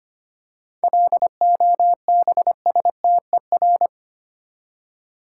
Morse code, heard as a keyed tone, about 25 wpm, 690 Hz. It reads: LOBSTER
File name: practice / 25wpm / Easy Words with Hints - 25wpm